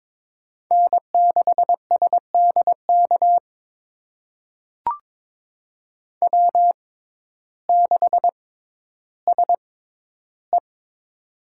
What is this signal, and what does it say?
Morse code, heard as a keyed tone, about 22 wpm, spaced as for 12 wpm, 700 Hz. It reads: N6SDK W6SE
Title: USA Call Signs - 6x Character Spacing - 22wpm